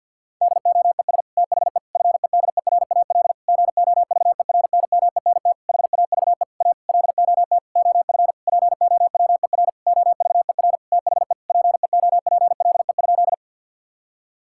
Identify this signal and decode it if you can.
Morse code, heard as a keyed tone, about 48 wpm, 700 Hz. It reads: DOES THE FEDERAL GOVERNMENT HAVE A LOT OF POWER OVER THE PEOPLE?
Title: Sentences from Top 1000 Words - Review Part 5 - 50wpm